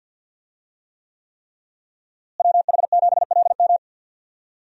Morse code, heard as a keyed tone, about 50 wpm, 700 Hz. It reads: WH7LM
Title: USA Call Signs - 5 Characters - Repeated 3x - 50wpm